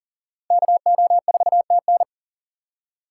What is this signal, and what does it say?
Morse code, heard as a keyed tone, about 40 wpm, 700 Hz. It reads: KO4TN
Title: USA Call Signs - 5 Characters - 4x Character Spacing - 40wpm